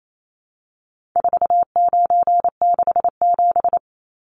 Morse code, heard as a keyed tone, about 28 wpm, 700 Hz. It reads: 4967
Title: Numbers - 4-Digit 28wpm